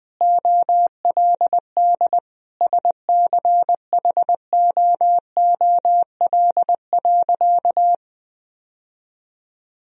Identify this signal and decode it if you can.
Morse code, heard as a keyed tone, about 21 wpm, 700 Hz. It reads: OLD SCHOOL.
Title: Sentences from Top 1000 Words - Review Part 1 - 20wpm